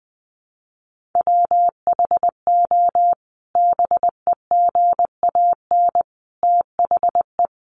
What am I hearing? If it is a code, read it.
Morse code, heard as a keyed tone, about 20 wpm, 700 Hz. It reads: WHO BEGAN THE
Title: Sentences from Top 1300 Words 20wpm